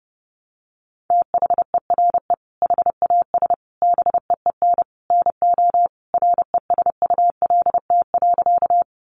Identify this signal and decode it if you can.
Morse code, heard as a keyed tone, about 30 wpm, 700 Hz. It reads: THERE HAS BEEN NO RESULT.
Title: Sentences from Top 700 Words 30wpm